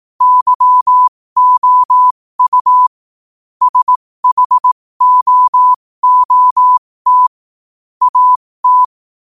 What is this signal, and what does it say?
Morse code, heard as a keyed tone, about 18 wpm, spaced as for 15 wpm, 1000 Hz. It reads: YOU SHOOT AT
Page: New quotes every day in morse code at 15 Words per minute.